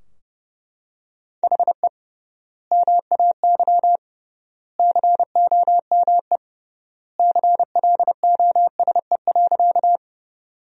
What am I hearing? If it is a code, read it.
Morse code, heard as a keyed tone, about 31 wpm, 700 Hz. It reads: HE MAY COME CLOSE.
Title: 3x Word-Spacing - Sentences from Top 300 Words - 30wpm